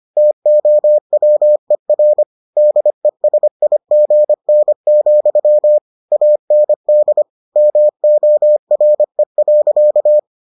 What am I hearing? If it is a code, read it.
Morse code, heard as a keyed tone, about 25 wpm, 600 Hz. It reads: TOWER DESIGN, AND MORE.